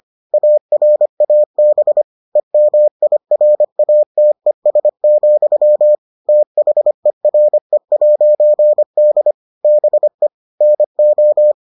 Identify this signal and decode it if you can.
Morse code, heard as a keyed tone, about 25 wpm, 600 Hz. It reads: ARAB EMIRATES, THERE'D BE NO